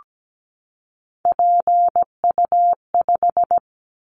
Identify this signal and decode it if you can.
Morse code, heard as a keyed tone, about 17 wpm, 700 Hz. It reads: PU5
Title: Callsign Prefixes for Top 100 DXCC - 17wpm